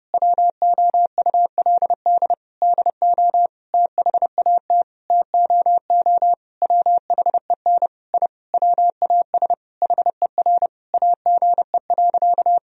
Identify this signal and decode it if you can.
Morse code, heard as a keyed tone, about 32 wpm, 700 Hz. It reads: WOULD DO THAT TOO WHEN I WAS HER AGE.
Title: Sentences from Top 1000 Words - Review Part 3 - 30wpm